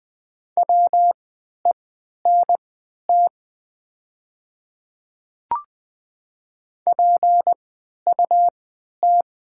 Morse code, heard as a keyed tone, about 20 wpm, spaced as for 11 wpm, 700 Hz. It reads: WENT PUT
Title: 3x Character Spacing - Top 200 Words - 20wpm